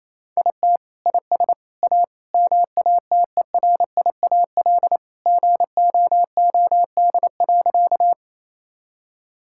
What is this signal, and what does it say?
Morse code, heard as a keyed tone, about 28 wpm, 700 Hz. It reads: IT IS A MATERIAL GOOD.
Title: Sentences from Top 1000 Words - Review Part 5 - 28wpm